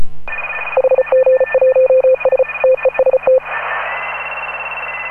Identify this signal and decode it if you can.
Morse code, heard as a keyed tone, about 34 wpm, 520 Hz, with noise. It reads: HG1S TEST
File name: AUDIO2018 / HA9A CW